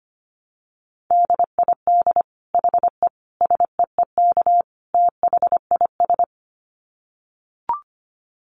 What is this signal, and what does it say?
Morse code, heard as a keyed tone, about 25 wpm, 700 Hz. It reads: DID HE SEEK THIS
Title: Dit Challenge 25wpm